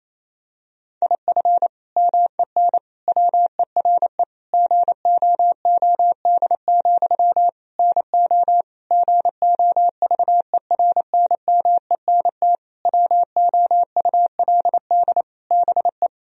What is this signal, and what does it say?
Morse code, heard as a keyed tone, about 28 wpm, 700 Hz. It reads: IF MEN WERE GOOD, NO GOVERNMENT WOULD BE
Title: Sentences from Top 1000 Words - Review Part 3 - 28wpm